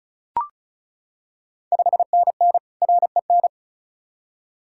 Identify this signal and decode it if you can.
Morse code, heard as a keyed tone, about 36 wpm, 700 Hz. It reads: 5NN REN
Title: Ontario QSO Party Exchange 35wpm